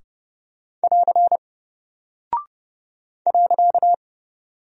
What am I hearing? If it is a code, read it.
Morse code, heard as a keyed tone, about 30 wpm, 700 Hz. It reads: <AR> .
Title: Single Character (Letter:Number:Punctuation:Prosign) Alt 30wpm